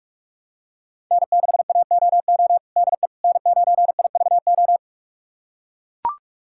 Morse code, heard as a keyed tone, about 41 wpm, 700 Hz. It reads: N6AOO DE N9IVO
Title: USA Callsign DE Callsign - 5 Characters - 45wpm